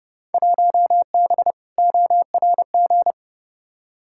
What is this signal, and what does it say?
Morse code, heard as a keyed tone, about 31 wpm, 700 Hz. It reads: 1B ORG
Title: Field Day Exchanges Encore - 30wpm